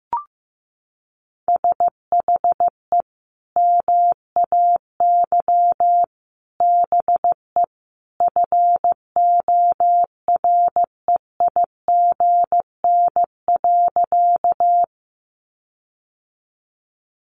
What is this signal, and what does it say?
Morse code, heard as a keyed tone, about 14 wpm, 700 Hz. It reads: SHE MAY BE FOREIGN.
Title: Sentences from Top 1400 Words 15wpm